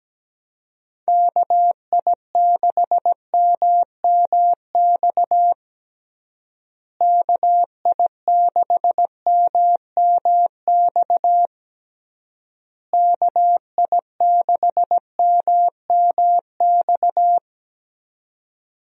Morse code, heard as a keyed tone, about 17 wpm, 700 Hz. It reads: KI6MMX KI6MMX KI6MMX
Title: USA Call Signs - 6 Characters - Repeated 3x - 17wpm